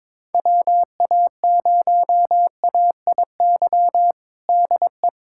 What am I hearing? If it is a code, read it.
Morse code, heard as a keyed tone, about 22 wpm, 700 Hz. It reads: WA0AIY DE
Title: USA Callsign DE Callsign 22wpm